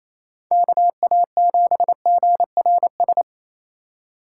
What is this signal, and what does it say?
Morse code, heard as a keyed tone, about 28 wpm, 700 Hz. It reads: KA7GRS
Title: USA Call Signs - 6 Characters - 2x Character Spacing - 28wpm